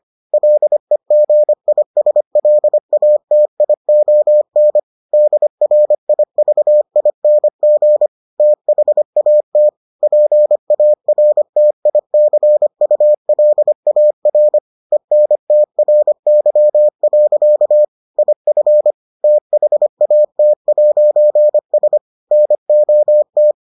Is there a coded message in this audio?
Yes: Morse code from 0.3 to 23.5 s, heard as a keyed tone, about 25 wpm, 600 Hz. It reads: LEGISLATION DRIVING THAT PARTICULAR ENTRY. IF THAT'S NOT